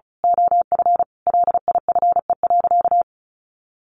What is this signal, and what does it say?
Morse code, heard as a keyed tone, about 34 wpm, 700 Hz. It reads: OF LIFE.
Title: A Sound Of Thunder 35wpm